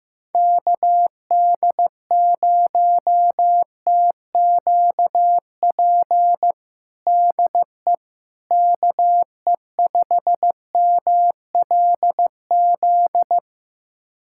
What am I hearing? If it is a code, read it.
Morse code, heard as a keyed tone, about 15 wpm, 700 Hz. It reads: KD0TQP DE KE5MLZ
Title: USA Callsign DE Callsign - 6 Characters - 15wpm